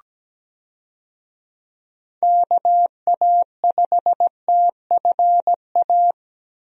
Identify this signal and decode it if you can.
Morse code, heard as a keyed tone, about 17 wpm, 700 Hz. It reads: KA5TFA
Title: USA Call Signs - 6 Characters - Repeated 2x - 17wpm